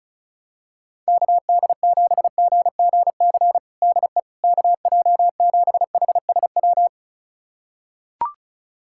Morse code, heard as a keyed tone, about 34 wpm, 700 Hz. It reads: KD7GGC DE KJ7HSW
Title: USA Callsign DE Callsign - 6 Characters - 35wpm